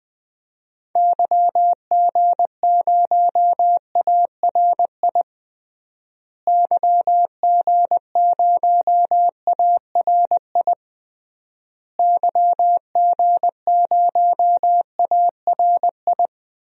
Morse code, heard as a keyed tone, about 21 wpm, 700 Hz. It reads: YG0ARI YG0ARI YG0ARI
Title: DXCC-100 Call Signs - 6 Characters - Repeated 3x - 20wpm